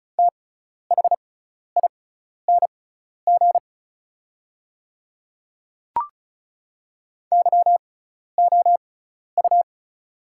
Morse code, heard as a keyed tone, about 34 wpm, spaced as for 12 wpm, 700 Hz. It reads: THING YOU
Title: 6x Character Spacing - Top 500 Words - 35wpm